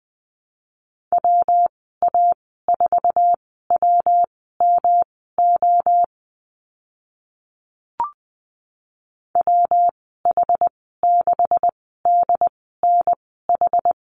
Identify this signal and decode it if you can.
Morse code, heard as a keyed tone, about 20 wpm, spaced as for 14 wpm, 700 Hz. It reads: WA4WMO WH6DNH
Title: 154 Lesson 25g - Letter p - Call Signs 20wpm